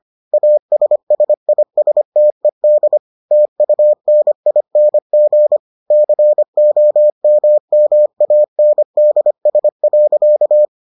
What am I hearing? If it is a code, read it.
Morse code, heard as a keyed tone, about 25 wpm, 600 Hz. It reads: ASSISTED TUNING COMMANDS.